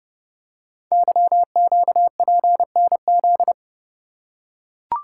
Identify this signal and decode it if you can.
Morse code, heard as a keyed tone, about 30 wpm, 690 Hz. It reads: YQPNZ
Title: Code Groups - 5 Letters - 30wpm